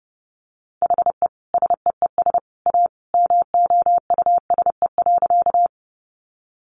Morse code, heard as a keyed tone, about 30 wpm, 700 Hz. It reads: HE SEES A MOUSE.
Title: Easy Phrases 30wpm